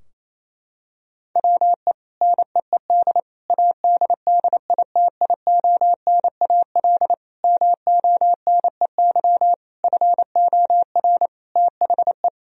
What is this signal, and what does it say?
Morse code, heard as a keyed tone, about 28 wpm, 700 Hz. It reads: WE NEED ADDITIONAL MONEY FOR THE
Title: Sentences from Top 1000 Words - Review Part 4 - 28wpm